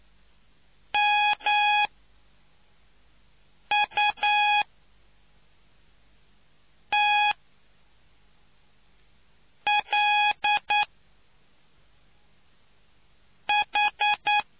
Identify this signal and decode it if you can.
Morse code, heard as a keyed tone, about 9 wpm, spaced as for 3 wpm, 830 Hz. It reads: MUTLH